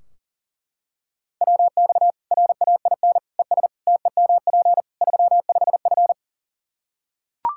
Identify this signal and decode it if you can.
Morse code, heard as a keyed tone, about 40 wpm, 700 Hz. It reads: WX RAIN ES TEMP 35F
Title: Practice QSO - Exchange 2 - Info - 40wpm